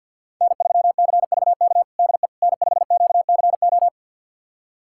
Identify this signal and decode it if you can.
Morse code, heard as a keyed tone, about 46 wpm, 700 Hz. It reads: N3CVK DE N5QCO
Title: USA Callsign DE Callsign - 5 Characters - 50wpm